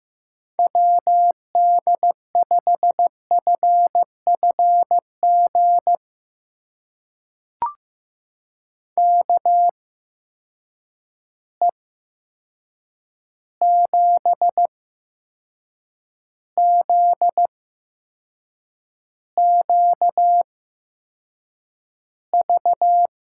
Morse code, heard as a keyed tone, about 15 wpm, spaced as for 6 wpm, 700 Hz. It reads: WD5FFG KE7ZQV
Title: USA Call Signs - 6 Characters - 8x Character Spacing - 15wpm